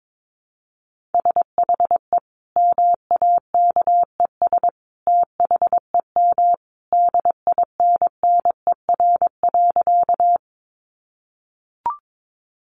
Morse code, heard as a keyed tone, about 22 wpm, 700 Hz. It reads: SHE MAKES THEM DINNER.
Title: Easy Phrases 22wpm